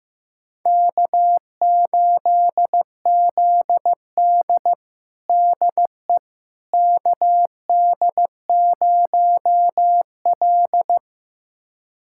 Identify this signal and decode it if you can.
Morse code, heard as a keyed tone, about 15 wpm, 700 Hz. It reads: K8ZD DE KD0L
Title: USA Callsign DE Callsign - 4 Characters - 15wpm